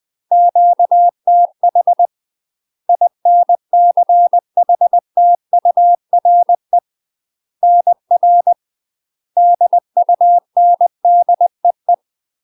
Morse code, heard as a keyed tone, about 20 wpm, 700 Hz. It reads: QTH INCHTURE NR DUNDEE
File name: UK MASTER QSOs TRAINING / 20WPM / UK MASTER QSOs 0067